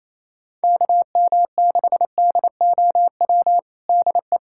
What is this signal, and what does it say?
Morse code, heard as a keyed tone, about 28 wpm, 700 Hz. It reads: KM6DOW DE
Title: USA Callsign DE Callsign - 6 Characters - 28wpm